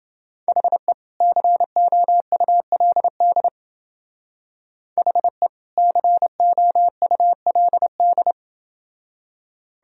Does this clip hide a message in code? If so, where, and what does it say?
Yes: Morse code from 0.5 to 8.3 s, heard as a keyed tone, about 28 wpm, 700 Hz. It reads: HE COULD HE COULD